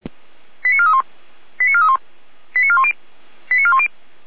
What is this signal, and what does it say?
Five-tone selective call (ZVEI), heard as quick runs of five tones, tones of about 70 ms. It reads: 89431 89431 89410 89410